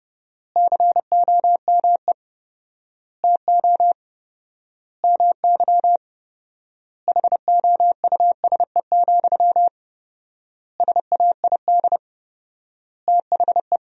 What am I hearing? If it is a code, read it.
Morse code, heard as a keyed tone, about 32 wpm, 700 Hz. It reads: COME TO MY HOUSE, SAID THE
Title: 4x Word-Spacing - Sentences from Top 400 Words - 30wpm